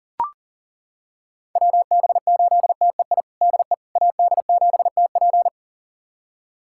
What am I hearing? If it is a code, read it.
Morse code, heard as a keyed tone, about 40 wpm, 700 Hz. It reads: WB8TEI DE AD7TP
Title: USA Callsign DE Callsign 40wpm